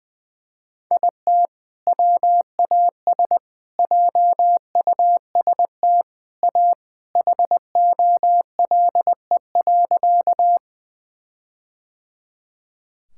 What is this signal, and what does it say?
Morse code, heard as a keyed tone, about 21 wpm, 700 Hz. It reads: IT WAS JUST A HOLE.
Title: Sentences from Top 1600 Words 20wpm